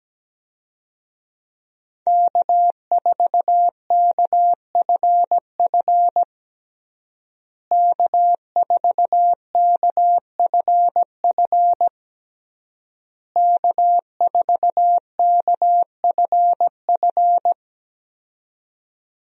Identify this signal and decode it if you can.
Morse code, heard as a keyed tone, about 17 wpm, 700 Hz. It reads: K4KFF K4KFF K4KFF
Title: USA Call Signs - 5 Characters - Repeated 3x - 17wpm